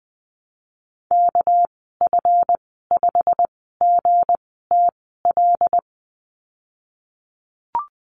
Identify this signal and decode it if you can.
Morse code, heard as a keyed tone, about 19 wpm, spaced as for 14 wpm, 700 Hz. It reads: KF5GTL
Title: 213 Lesson 34g - Number 8 - Call Signs 20wpm